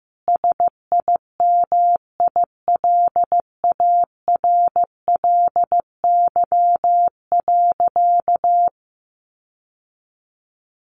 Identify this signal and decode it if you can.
Morse code, heard as a keyed tone, about 15 wpm, 700 Hz. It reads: SIMILARLY.
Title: Sentences from Top 1400 Words 15wpm